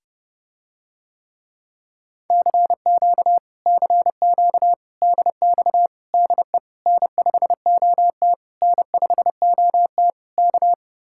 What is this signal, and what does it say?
Morse code, heard as a keyed tone, about 30 wpm, 700 Hz. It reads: CQ CQ DX DE N5OT N5OT K